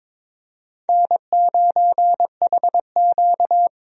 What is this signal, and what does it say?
Morse code, heard as a keyed tone, about 21 wpm, 700 Hz. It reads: N9HQ
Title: USA Call Signs - 4 Characters - Repeated 3x - 22wpm